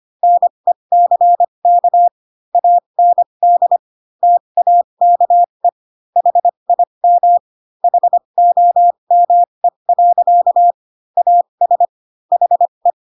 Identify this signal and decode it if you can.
Morse code, heard as a keyed tone, about 25 wpm, spaced as for 22 wpm, 700 Hz. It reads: NECK AND TAKE HIM HOME. AS HE
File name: Pinocchio / Pinocchio ch3 18 25 ik1hge